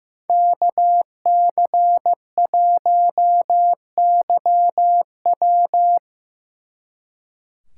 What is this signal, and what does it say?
Morse code, heard as a keyed tone, about 17 wpm, 700 Hz. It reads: KC1YW
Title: Speed Racing - USA Call Signs 15wpm